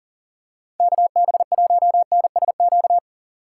Morse code, heard as a keyed tone, about 40 wpm, 700 Hz. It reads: KB1NSQ
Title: USA Call Signs - 2x Character Spacing - 40wpm